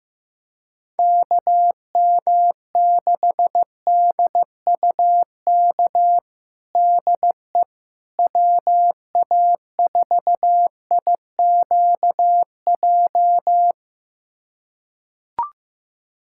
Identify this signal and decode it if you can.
Morse code, heard as a keyed tone, about 15 wpm, 700 Hz. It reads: KM6DUK DE WA4IQJ
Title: USA Callsign DE Callsign - 6 Characters - 15wpm